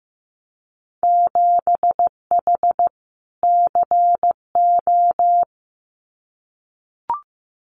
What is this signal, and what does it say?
Morse code, heard as a keyed tone, about 15 wpm, 700 Hz. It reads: 7H CO
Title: Winter Field Day Exchanges Encore 15wpm